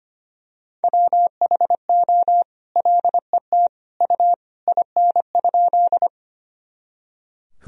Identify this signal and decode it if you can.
Morse code, heard as a keyed tone, about 25 wpm, 700 Hz. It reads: WHO LET U IN?
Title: Sentences from Top 300 Words and Abbreviations 25wpm